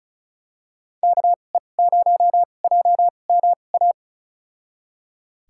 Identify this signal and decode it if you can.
Morse code, heard as a keyed tone, about 36 wpm, spaced as for 25 wpm, 700 Hz. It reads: KE0JMA
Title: USA Call Signs - 6 Characters - 2x Character Spacing - 35wpm